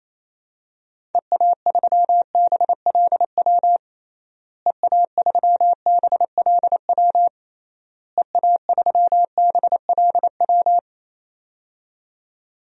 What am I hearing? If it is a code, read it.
Morse code, heard as a keyed tone, about 28 wpm, 700 Hz. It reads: EA3BLW EA3BLW EA3BLW
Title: DXCC-100 Call Signs - 6 Characters - Repeated 3x - 28wpm